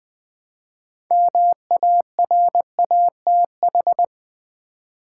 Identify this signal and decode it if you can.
Morse code, heard as a keyed tone, about 20 wpm, 700 Hz. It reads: MARATH
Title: Build A Word - 3 Letters At A Time - 8-Letter Words - 20wpm